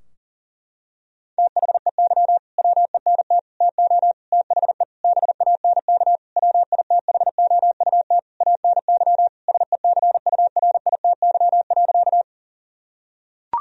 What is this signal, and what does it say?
Morse code, heard as a keyed tone, about 37 wpm, 700 Hz. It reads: THEY WENT TO THE BANK WITHOUT ANY SECURITY.
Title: Sentences from Top 1000 Words - Review Part 4 - 40wpm